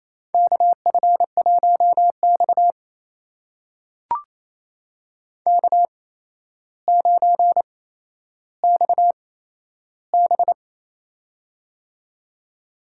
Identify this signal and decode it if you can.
Morse code, heard as a keyed tone, about 27 wpm, spaced as for 12 wpm, 700 Hz. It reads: KF1X K9XB
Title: USA Call Signs - 4 Characters - 8x Character Spacing - 28wpm